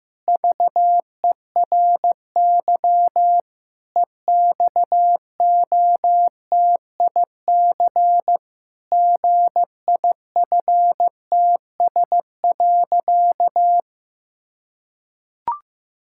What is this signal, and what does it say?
Morse code, heard as a keyed tone, about 15 wpm, 700 Hz. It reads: VERY EXOTIC GIFTS.
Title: Pangram Challenge v2 - 15wpm